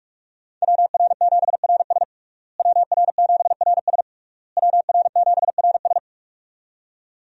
Morse code, heard as a keyed tone, about 41 wpm, 700 Hz. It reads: WR7RS WR7RS WR7RS
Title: USA Call Signs - 5 Characters - Repeated 3x - 45wpm